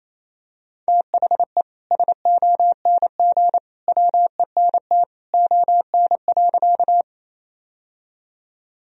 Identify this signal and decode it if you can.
Morse code, heard as a keyed tone, about 27 wpm, 700 Hz. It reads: THE SONG WENT ON.
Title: Sentences from Top 1000 Words - Review Part 2 - 28wpm